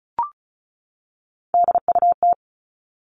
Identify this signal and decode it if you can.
Morse code, heard as a keyed tone, about 34 wpm, 700 Hz. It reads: DUT
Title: All Cut Numbers - 3-Digit 35wpm